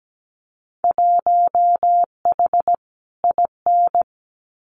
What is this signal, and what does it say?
Morse code, heard as a keyed tone, about 17 wpm, 700 Hz. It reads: 1H IN
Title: Winter Field Day Exchanges 17wpm